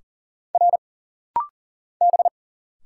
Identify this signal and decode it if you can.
Morse code, heard as a keyed tone, about 40 wpm, 700 Hz. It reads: R B
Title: Single Letter-Number Alt - Rapid-Fire 40wpm